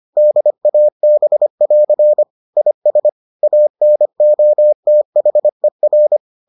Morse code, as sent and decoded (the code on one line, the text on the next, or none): -.. .- -... .-.-. / .. ... / .- -. --- - .... .-.
DAB<AR> IS ANOTHER